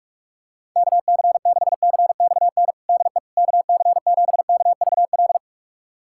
Morse code, heard as a keyed tone, about 44 wpm, 700 Hz. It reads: KC6CXN DE KC7KVL
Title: USA Callsign DE Callsign - 6 Characters - 45wpm